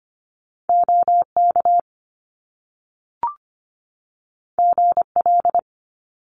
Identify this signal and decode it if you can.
Morse code, heard as a keyed tone, about 25 wpm, 700 Hz. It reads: OK GL
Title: Top Two Letter Combinations in Words 25wpm